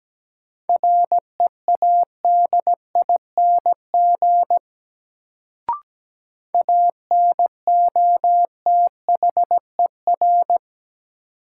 Morse code, heard as a keyed tone, about 17 wpm, 700 Hz. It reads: READING ANOTHER
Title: Sentences from Top 600 Words - New Word Drill - 17wpm